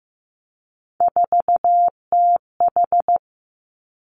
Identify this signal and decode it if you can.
Morse code, heard as a keyed tone, about 15 wpm, 700 Hz. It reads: 4TH
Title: Three Challenging Letter-Number Combinations 15wpm